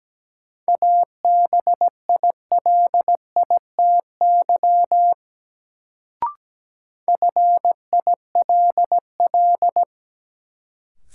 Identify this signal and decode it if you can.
Morse code, heard as a keyed tone, about 17 wpm, 700 Hz. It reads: ABILITY FILL
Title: Sentences from Top 1000 Words - Review Part 3 - New Word Drill - 17wpm